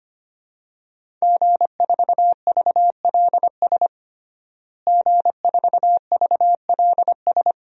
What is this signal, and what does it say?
Morse code, heard as a keyed tone, about 25 wpm, 700 Hz. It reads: G4VLS G4VLS
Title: DXCC-100 Call Signs - 5 Characters - Repeated 2x - 25wpm